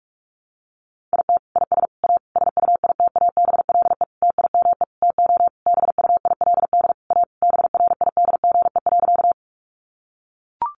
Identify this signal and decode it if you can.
Morse code, heard as a keyed tone, about 44 wpm, 700 Hz. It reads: IT IS A SUITABLE TIME TO BUILD A BRIDGE.
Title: Sentences from Top 1400 Words 45wpm